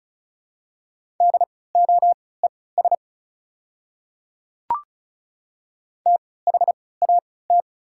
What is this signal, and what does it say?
Morse code, heard as a keyed tone, about 36 wpm, spaced as for 20 wpm, 700 Hz. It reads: DOES THAT
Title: 3x Character Spacing - Top 200 Words - 35wpm